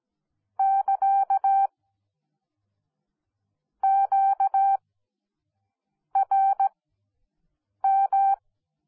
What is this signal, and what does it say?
Morse code, heard as a keyed tone, about 17 wpm, spaced as for 6 wpm, 780 Hz. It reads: <KA> QRM